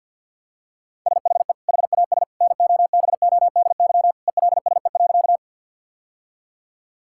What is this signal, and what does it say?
Morse code, heard as a keyed tone, about 50 wpm, 700 Hz. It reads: SHE HAS NOBODY ELSE.
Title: Sentences from Top 1600 Words 50wpm